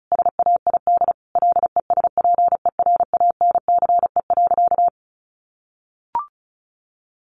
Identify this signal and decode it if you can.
Morse code, heard as a keyed tone, about 34 wpm, 700 Hz. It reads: SAID LESPERANCE.
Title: A Sound Of Thunder 35wpm